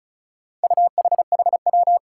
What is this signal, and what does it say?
Morse code, heard as a keyed tone, about 34 wpm, 700 Hz. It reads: UHHW
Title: Four Challenging Letter-Number Combinations 35wpm